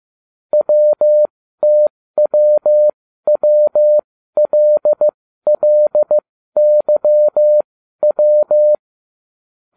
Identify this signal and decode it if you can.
Morse code, heard as a keyed tone, about 15 wpm, spaced as for 12 wpm, 600 Hz. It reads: WTWWLLYW